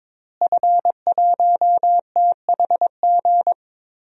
Morse code, heard as a keyed tone, about 22 wpm, 700 Hz. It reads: F1THG
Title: DXCC-100 Call Signs - 5 Characters - Repeated 3x - 22wpm